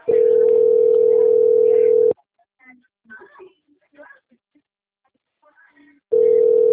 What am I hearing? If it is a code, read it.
Ringback tone (North America), 440 + 480 Hz, with speech.
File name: out- - - - - .27641